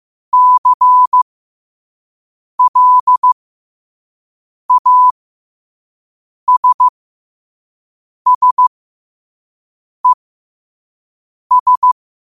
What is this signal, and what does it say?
Morse code, heard as a keyed tone, about 15 wpm, spaced as for 5 wpm, 1000 Hz. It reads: CLASSES
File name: qotdfiles / 5wpm